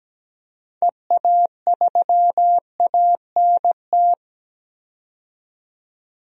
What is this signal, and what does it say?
Morse code, heard as a keyed tone, about 17 wpm, 700 Hz. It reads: EA3ANT
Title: DXCC-100 Call Signs - 6 Characters - 17wpm